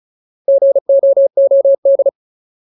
Morse code, heard as a keyed tone, about 34 wpm, 550 Hz. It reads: GOOD
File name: Ultimate Word List / 35WPM Ultimate